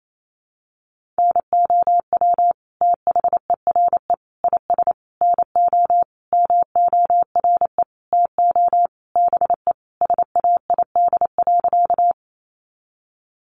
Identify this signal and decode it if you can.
Morse code, heard as a keyed tone, about 28 wpm, 700 Hz. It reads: NOW THERE IS NO MORE TO BE SAID.
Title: Sentences from Top 100 Words 28wpm